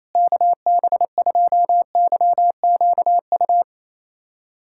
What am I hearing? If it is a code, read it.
Morse code, heard as a keyed tone, about 28 wpm, 700 Hz. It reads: KB2YQU
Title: USA Callsign DE Callsign 28wpm